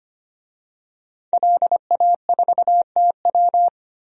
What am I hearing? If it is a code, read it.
Morse code, heard as a keyed tone, about 25 wpm, 700 Hz. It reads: LA4TW
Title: DXCC-100 Call Signs - 5 Characters - 25wpm